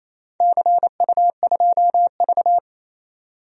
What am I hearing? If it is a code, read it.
Morse code, heard as a keyed tone, about 28 wpm, 700 Hz. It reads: CU2V
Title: DXCC-100 Call Signs - 4 Characters - 28wpm